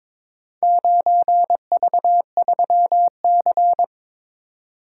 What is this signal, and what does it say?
Morse code, heard as a keyed tone, about 22 wpm, 700 Hz. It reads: 9V3C
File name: practice / 22wpm / Four Letter-Number Combinations - 22wpm